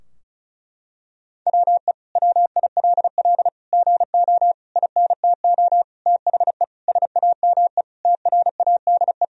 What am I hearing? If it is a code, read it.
Morse code, heard as a keyed tone, about 36 wpm, 700 Hz. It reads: WE WILL GO INTO THE SAME TRADE
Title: Sentences from Top 1000 Words - Review Part 4 - 35wpm